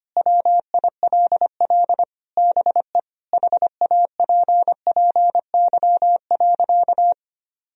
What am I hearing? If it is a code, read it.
Morse code, heard as a keyed tone, about 25 wpm, 700 Hz. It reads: WILL BE HAPPY.
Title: Sentences from Top 1000 Words - Review Part 2 - 25wpm